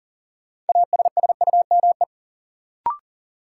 Morse code, heard as a keyed tone, about 40 wpm, 700 Hz. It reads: ASSUME
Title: Sentences from Top 1500 Words - New Word Drill - 40wpm